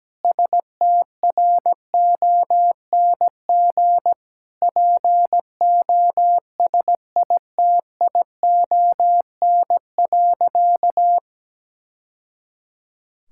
Morse code, heard as a keyed tone, about 17 wpm, 700 Hz. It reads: STRONG POSITION.
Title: Sentences from Top 1000 Words - Review Part 5 - 17wpm